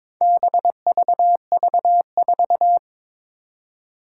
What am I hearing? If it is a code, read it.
Morse code, heard as a keyed tone, about 23 wpm, 700 Hz. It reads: BVV4
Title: Four Challenging Letter-Number Combinations 22wpm